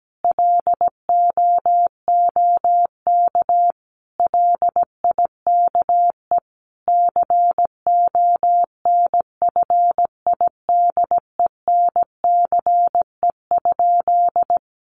Morse code, heard as a keyed tone, about 17 wpm, 700 Hz. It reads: LOOK LIKE CONFIDENCE?
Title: Sentences from Top 1500 Words 17wpm